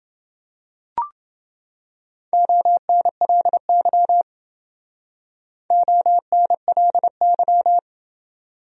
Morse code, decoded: ONLY ONLY

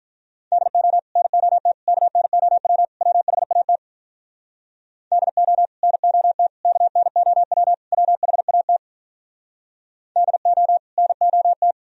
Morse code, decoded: DO NOT KNOW WHAT DO NOT KNOW WHAT DO NOT

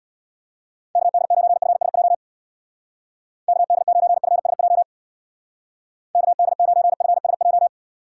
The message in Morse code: KD9FSJ KD9FSJ KD9FSJ

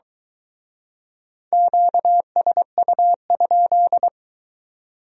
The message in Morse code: QSU?

